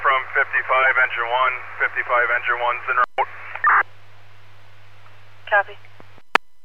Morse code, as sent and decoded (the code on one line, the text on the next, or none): none